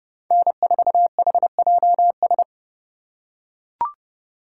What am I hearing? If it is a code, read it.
Morse code, heard as a keyed tone, about 29 wpm, 700 Hz. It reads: N4HJS